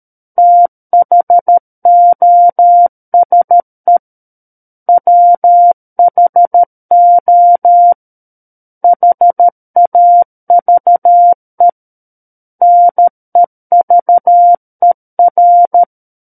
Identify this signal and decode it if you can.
Morse code, heard as a keyed tone, about 13 wpm, 700 Hz. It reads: THOSE WHO HAVE NEVER
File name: Book One / 04-Chapter 4 - 13 WPM